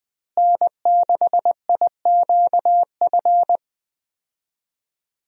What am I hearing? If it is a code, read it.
Morse code, heard as a keyed tone, about 20 wpm, 700 Hz. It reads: N6IQF
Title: USA Call Signs - 5 Characters - Repeated 3x - 20wpm